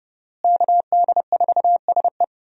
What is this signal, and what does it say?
Morse code, heard as a keyed tone, about 31 wpm, 700 Hz. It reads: KD4SE
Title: USA Call Signs - 5 Characters - Repeated 2x - 30wpm